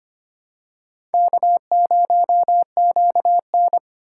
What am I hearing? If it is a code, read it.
Morse code, heard as a keyed tone, about 25 wpm, 700 Hz. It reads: K0QN
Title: USA Call Signs - 4 Characters - 2x Character Spacing - 25wpm